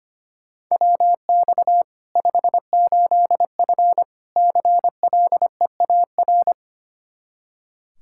Morse code, heard as a keyed tone, about 25 wpm, 700 Hz. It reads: WX 58F CLEAR
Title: Practice QSO - Exchange 2 - Info - 25wpm